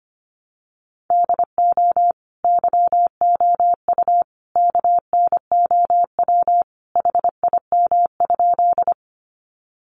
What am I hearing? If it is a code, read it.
Morse code, heard as a keyed tone, about 25 wpm, 700 Hz. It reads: DO YOU KNOW HIM?
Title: Sentences from Top 100 Words 25wpm